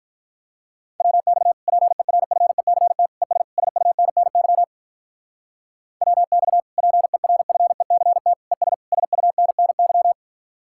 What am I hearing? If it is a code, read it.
Morse code, heard as a keyed tone, about 46 wpm, 700 Hz. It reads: WX PERFECT ES SUNNY WX PERFECT ES SUNNY